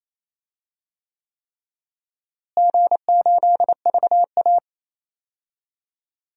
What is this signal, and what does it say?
Morse code, heard as a keyed tone, about 28 wpm, 700 Hz. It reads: G8VA
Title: DXCC-100 Call Signs - 4 Characters - Repeated 3x - 28wpm